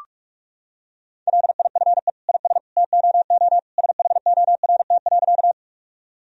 Morse code, heard as a keyed tone, about 46 wpm, 700 Hz. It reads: LIFE IS TOO SHORT.
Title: Sentences from Top 1000 Words - Review Part 2 - 45wpm